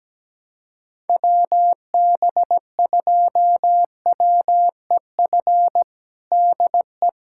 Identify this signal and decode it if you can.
Morse code, heard as a keyed tone, about 17 wpm, 700 Hz. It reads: WB2WEF DE
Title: USA Callsign DE Callsign 17wpm